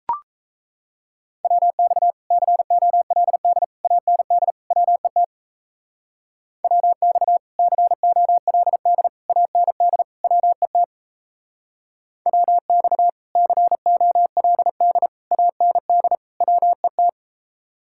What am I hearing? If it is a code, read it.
Morse code, heard as a keyed tone, about 33 wpm, 700 Hz. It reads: WX COLD AND WET WX COLD AND WET WX COLD AND WET
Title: Speed-Racing Common QSO Phrases Encore - 28wpm